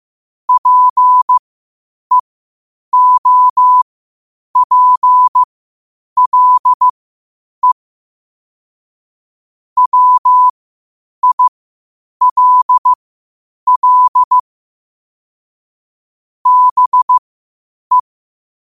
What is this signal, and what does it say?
Morse code, heard as a keyed tone, about 15 wpm, spaced as for 8 wpm, 1000 Hz. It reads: PEOPLE WILL BE
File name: qotdfiles / 8wpm